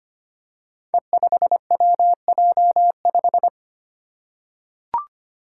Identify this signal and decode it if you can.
Morse code, heard as a keyed tone, about 25 wpm, 700 Hz. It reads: E5WJ5
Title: Five Challenging Letter-Number Combinations 25wpm